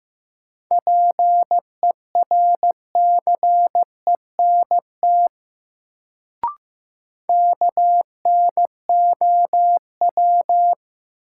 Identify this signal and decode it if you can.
Morse code, heard as a keyed tone, about 15 wpm, 700 Hz. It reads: PERCENT KNOW
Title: Sets of 2 Words - New Word Drill - 15wpm